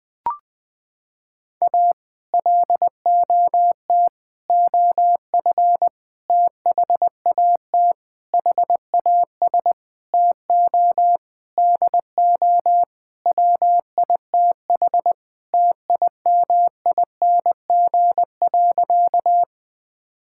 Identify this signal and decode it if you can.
Morse code, heard as a keyed tone, about 20 wpm, 700 Hz. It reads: A LOT OF THAT HAS TO DO WITH TIMING.
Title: Sentences with 8-Word Phrases - 20wpm